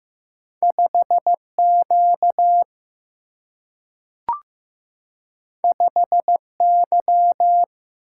Morse code, heard as a keyed tone, about 15 wpm, 700 Hz. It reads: 5Q 5Y
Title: Single-Letter Single-Number 15wpm